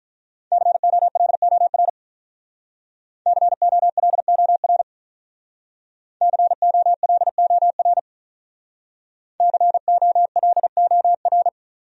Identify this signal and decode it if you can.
Morse code, heard as a keyed tone, about 43 wpm, 700 Hz. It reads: COLOR COLOR COLOR COLOR
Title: Speed-Racing Top 500 Words 35wpm